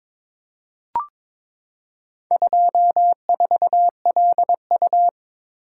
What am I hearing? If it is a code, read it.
Morse code, heard as a keyed tone, about 22 wpm, 700 Hz. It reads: 24LU
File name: practice / 22wpm / Four Letter-Number Combinations - 22wpm